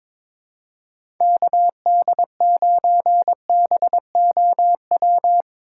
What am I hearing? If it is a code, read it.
Morse code, heard as a keyed tone, about 22 wpm, 700 Hz. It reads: KD9BOW
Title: USA Call Signs - 2x Character Spacing - 22wpm